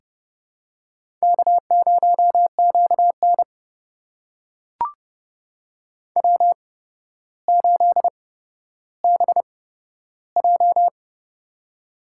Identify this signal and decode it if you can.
Morse code, heard as a keyed tone, about 30 wpm, spaced as for 13 wpm, 700 Hz. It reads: K0QN W8BJ
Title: USA Call Signs - 4 Characters - 8x Character Spacing - 30wpm